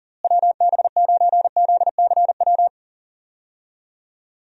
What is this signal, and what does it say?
Morse code, heard as a keyed tone, about 40 wpm, 700 Hz. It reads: WB9ZCW
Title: USA Call Signs - 6 Characters - Repeated 2x - 40wpm